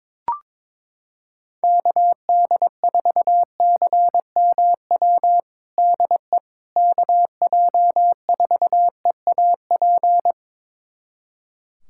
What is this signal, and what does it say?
Morse code, heard as a keyed tone, about 22 wpm, 700 Hz. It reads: KD4CMW DE KJ4EAP
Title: USA Callsign DE Callsign 22wpm